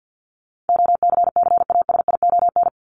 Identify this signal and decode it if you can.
Morse code, heard as a keyed tone, about 54 wpm, 700 Hz. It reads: OCCASION
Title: Top 1501-1750 Words 50wpm